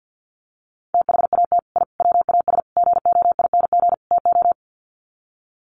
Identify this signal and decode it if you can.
Morse code, heard as a keyed tone, about 50 wpm, 700 Hz. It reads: THAT I WAS GOING TO
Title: Sets of 5 Words Encore 50wpm